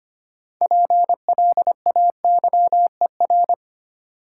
Morse code, heard as a keyed tone, about 25 wpm, 700 Hz. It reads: PLAYER
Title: Sentences from Top 1000 Words - Review Part 4 - New Word Drill - 25wpm